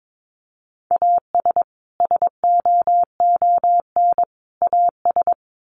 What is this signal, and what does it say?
Morse code, heard as a keyed tone, about 22 wpm, 700 Hz. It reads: AS SOON AS
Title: Sets of 3 Words 22wpm